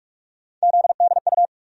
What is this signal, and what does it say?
Morse code, heard as a keyed tone, about 46 wpm, 700 Hz. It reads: ZDU
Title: USA Call Sign Postfixes - 3 Letters 45wpm